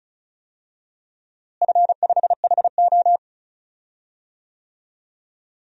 Morse code, heard as a keyed tone, about 34 wpm, 700 Hz. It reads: F5HO
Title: DXCC-100 Call Signs - 4 Characters - Repeated 2x - 35wpm